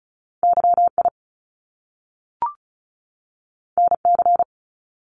Morse code, heard as a keyed tone, about 36 wpm, 700 Hz. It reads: YI NC
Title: Top Two Letter Combinations in Words 35wpm